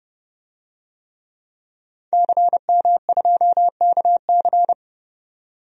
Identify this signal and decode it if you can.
Morse code, heard as a keyed tone, about 32 wpm, 700 Hz. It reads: CM2KC
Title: DXCC-100 Call Signs - 5 Characters - 30wpm